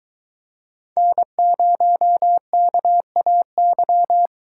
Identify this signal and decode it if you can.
Morse code, heard as a keyed tone, about 23 wpm, 700 Hz. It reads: N0KAY